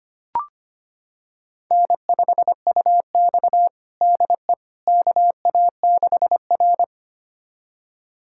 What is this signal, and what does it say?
Morse code, heard as a keyed tone, about 25 wpm, 700 Hz. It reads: N5UX DE KA6R
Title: USA Callsign DE Callsign - 4 Characters - 25wpm